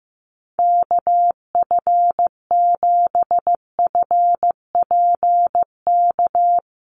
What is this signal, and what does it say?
Morse code, heard as a keyed tone, about 15 wpm, 700 Hz. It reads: KF7FPK
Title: USA Call Signs Repeated 2x 15wpm